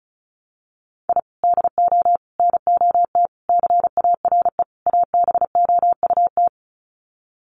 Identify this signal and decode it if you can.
Morse code, heard as a keyed tone, about 34 wpm, 700 Hz. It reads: I DO NOT CARE ABOUT
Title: Sets of 5 Words Encore2 35wpm